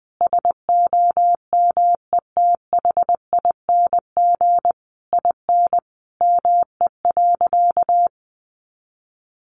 Morse code, heard as a keyed tone, about 19 wpm, 700 Hz. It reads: SOMETHING IN ME.
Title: Sentences from Top 600 Words 20wpm